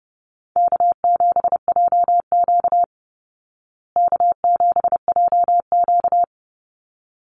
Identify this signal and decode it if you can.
Morse code, heard as a keyed tone, about 30 wpm, 700 Hz. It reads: K7JQ K7JQ
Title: Easy Call Signs Repeated 2x - 30wpm